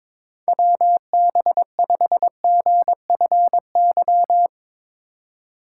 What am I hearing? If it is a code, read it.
Morse code, heard as a keyed tone, about 22 wpm, 700 Hz. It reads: WB5GFY
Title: USA Call Signs - 3x Character Spacing - 22wpm